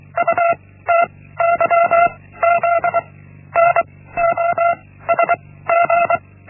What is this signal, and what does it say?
Morse code, heard as a keyed tone, about 24 wpm, 680 Hz. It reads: U T Y Z N O S G
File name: Linksmakalnis-1